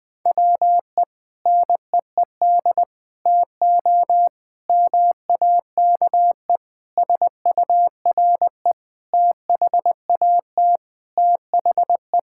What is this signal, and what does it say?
Morse code, heard as a keyed tone, about 20 wpm, 700 Hz. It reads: WE NEED TO MAKE SURE THAT THE
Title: Sentences with 5-Word Phrases - 20wpm